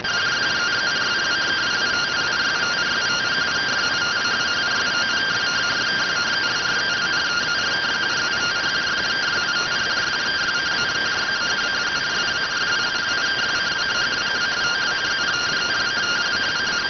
Two-tone teleprinter signal, about 170 Hz shift.